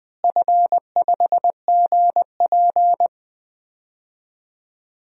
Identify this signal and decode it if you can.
Morse code, heard as a keyed tone, about 20 wpm, 700 Hz. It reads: F5GP
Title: DXCC-100 Call Signs - 4 Characters - Repeated 2x - 20wpm